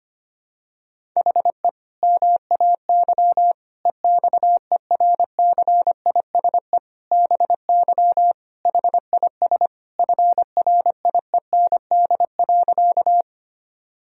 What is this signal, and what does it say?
Morse code, heard as a keyed tone, about 25 wpm, 700 Hz. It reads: HE MAY EXERCISE BY HIS FRIEND.
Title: Sentences from Top 1000 Words - Review Part 4 - 25wpm